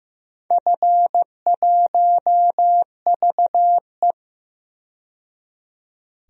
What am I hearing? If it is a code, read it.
Morse code, heard as a keyed tone, about 15 wpm, 700 Hz. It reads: F1VE
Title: DXCC-100 Call Signs - 4 Characters - 15wpm